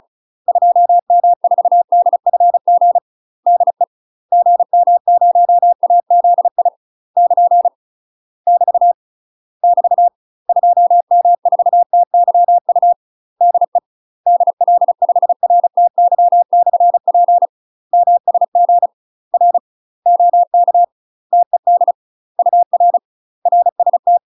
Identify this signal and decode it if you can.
Morse code, heard as a keyed tone, about 36 wpm, 700 Hz. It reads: 2M4DFG DE GM0AZI <KN> <BT> <BT> 2M4TYU DE DL5RTY/P MSG R OK TED UR RST
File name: UK MASTER QSOs TRAINING / 35WPM / UK MASTER QSOs 0002